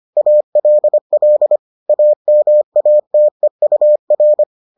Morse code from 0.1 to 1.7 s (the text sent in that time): ALL